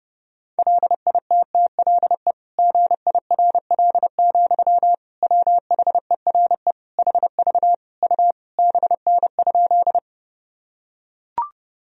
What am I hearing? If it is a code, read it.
Morse code, heard as a keyed tone, about 30 wpm, 700 Hz. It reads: LITTLE GIRL, WHERE HV U BN?
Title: Sentences from Top 400 Words and Abbreviations 30wpm